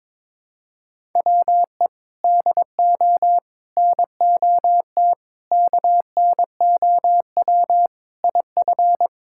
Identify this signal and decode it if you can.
Morse code, heard as a keyed tone, about 22 wpm, 700 Hz. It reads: WE DO NOT KNOW IF
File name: practice / 22wpm / Sentences with 4-Word Phrases - 22wpm